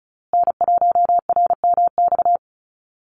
Morse code, heard as a keyed tone, about 34 wpm, 700 Hz. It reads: N1RMX
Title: USA Call Signs Repeated 2x 35wpm